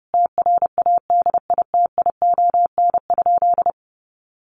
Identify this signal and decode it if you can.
Morse code, heard as a keyed tone, about 30 wpm, 700 Hz. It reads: TRADITION?
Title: Sentences from Top 1500 Words 30wpm